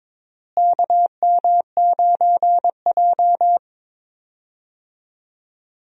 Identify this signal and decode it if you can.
Morse code, heard as a keyed tone, about 22 wpm, 700 Hz. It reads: KM9J
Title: USA Call Signs - 4 Characters - Repeated 3x - 22wpm